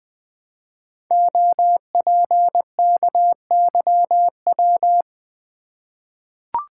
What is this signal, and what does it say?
Morse code, heard as a keyed tone, about 20 wpm, 700 Hz. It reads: OPKYW
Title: Code Groups - 5 Letter:Numbers - 20wpm